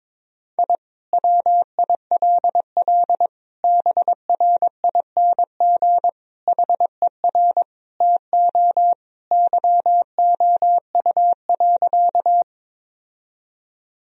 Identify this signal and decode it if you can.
Morse code, heard as a keyed tone, about 22 wpm, 700 Hz. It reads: I WILL BRING HER TO YOU.
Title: Sentences from Top 1000 Words - Review Part 3 - 22wpm